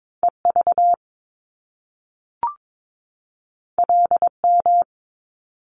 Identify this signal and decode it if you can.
Morse code, heard as a keyed tone, about 22 wpm, 700 Hz. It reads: EV LM
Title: Top Two Letter Combinations in Words 22wpm